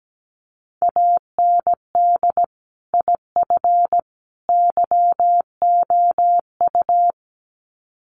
Morse code, heard as a keyed tone, about 17 wpm, 700 Hz. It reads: AND IF YOU
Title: Sets of 3 Words 17wpm